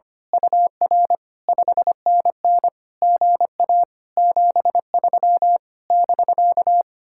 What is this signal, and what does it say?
Morse code, heard as a keyed tone, about 24 wpm, 700 Hz. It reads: UR 5NN GA 73 <BK>
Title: Speed-Racing Common QSO Phrases - 25wpm